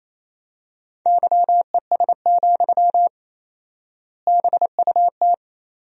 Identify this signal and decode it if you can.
Morse code, heard as a keyed tone, about 28 wpm, 700 Hz. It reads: YES, BUT
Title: 4x Word-Spacing - Sentences from Top 500 Words - 28wpm